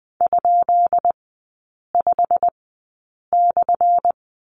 Morse code, heard as a keyed tone, about 20 wpm, spaced as for 8 wpm, 700 Hz. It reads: ?5/
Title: 178 Lesson 29e - Slash - Three Characters 20wpm